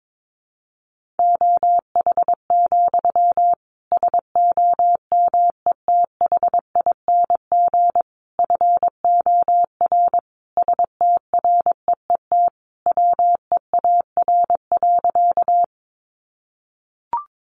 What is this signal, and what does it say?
Morse code, heard as a keyed tone, about 22 wpm, 700 Hz. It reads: OH, SOMETHING FOR STREET WEAR.
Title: Sentences from Top 1500 Words 22wpm